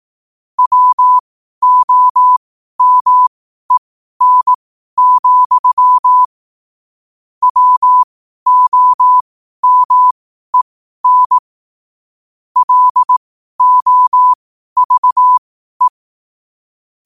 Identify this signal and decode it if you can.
Morse code, heard as a keyed tone, about 18 wpm, spaced as for 12 wpm, 1000 Hz. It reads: WOMEN, WOMEN LOVE
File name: qotdfiles / 12wpm